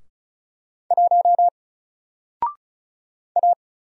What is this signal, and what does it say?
Morse code, heard as a keyed tone, about 36 wpm, 700 Hz. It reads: 1 A
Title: Single Letter-Number 35wpm